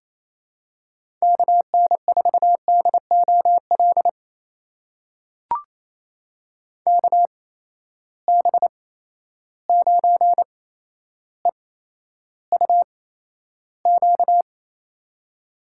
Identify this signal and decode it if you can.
Morse code, heard as a keyed tone, about 28 wpm, spaced as for 12 wpm, 700 Hz. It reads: KN4DOL KB9EUQ
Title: USA Call Signs - 6 Characters - 8x Character Spacing - 28wpm